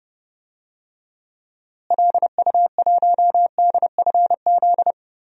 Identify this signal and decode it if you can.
Morse code, heard as a keyed tone, about 30 wpm, 700 Hz. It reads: LU1DFZ
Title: DXCC-100 Call Signs - 6 Characters - Repeated 2x - 30wpm